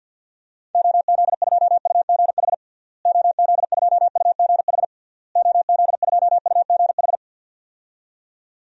Morse code, heard as a keyed tone, about 47 wpm, 700 Hz. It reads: OZ2UGH OZ2UGH OZ2UGH
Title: DXCC-100 Call Signs - 6 Characters - Repeated 3x - 50wpm